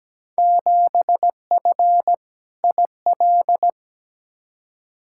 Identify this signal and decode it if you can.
Morse code, heard as a keyed tone, about 17 wpm, 700 Hz. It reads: 7F IL
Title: Field Day Exchanges Encore - 17wpm